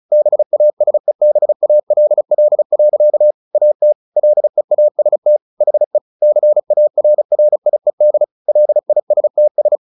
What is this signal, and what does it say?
Morse code, heard as a keyed tone, about 34 wpm, 600 Hz. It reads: BASEBALL. AT LEAST HE CARRIED LISTS
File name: the old man and the sea / the old man and the sea 1 35wpm